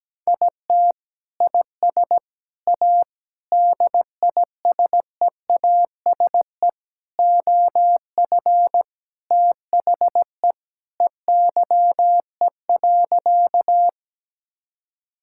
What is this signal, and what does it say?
Morse code, heard as a keyed tone, about 17 wpm, 700 Hz. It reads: IT IS A DISEASE OF THE EYE.
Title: Sentences from Top 1000 Words - Review Part 6 - 17wpm